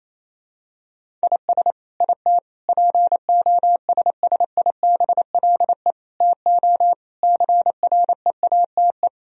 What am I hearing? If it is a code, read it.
Morse code, heard as a keyed tone, about 28 wpm, 700 Hz. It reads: IS IT POSSIBLE TO CREATE
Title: Sentences from Top 1000 Words - Review Part 3 - 28wpm